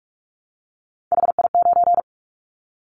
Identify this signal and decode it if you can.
Morse code, heard as a keyed tone, about 46 wpm, 700 Hz. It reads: HI9
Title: Callsign Prefixes for Top 100 DXCC - 45wpm